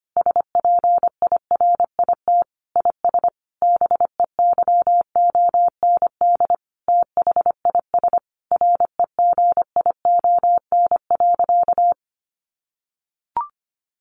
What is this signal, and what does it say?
Morse code, heard as a keyed tone, about 25 wpm, 700 Hz. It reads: SPIRIT IS BEYOND THIS REGION.
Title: Sentences from Top 1500 Words 25wpm